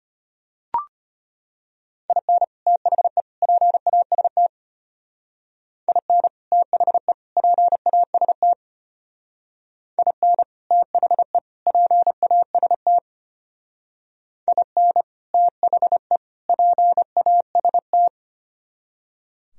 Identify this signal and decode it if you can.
Morse code, heard as a keyed tone, about 32 wpm, 700 Hz. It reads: IN THE PAST IN THE PAST IN THE PAST IN THE PAST